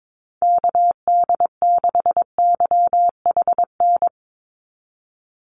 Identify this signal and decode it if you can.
Morse code, heard as a keyed tone, about 22 wpm, 700 Hz. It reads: KD6YHN
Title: USA Call Signs 22wpm